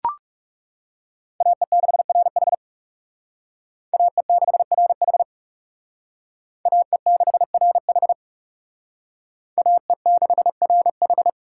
Morse code, heard as a keyed tone, about 38 wpm, 700 Hz. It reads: AE6RH AE6RH AE6RH AE6RH